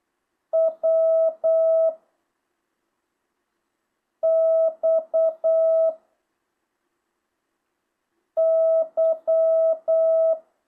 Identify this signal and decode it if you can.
Morse code, heard as a keyed tone, about 8 wpm, 650 Hz. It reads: W X Y
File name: jh8cbh